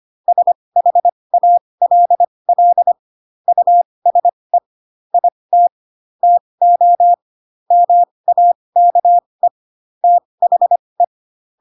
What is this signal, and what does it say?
Morse code, heard as a keyed tone, about 25 wpm, spaced as for 20 wpm, 700 Hz. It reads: SHALL USE IT TO MAKE THE
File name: Pinocchio / Pinocchio ch1 15 25 ik1hge